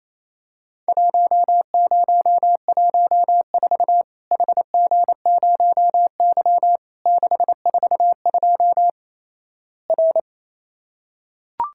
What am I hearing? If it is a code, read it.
Morse code, heard as a keyed tone, about 28 wpm, 700 Hz. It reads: 1014 HG0Y 642 R
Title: Worked All Europe DX Contest - QTC Exchange - 28wpm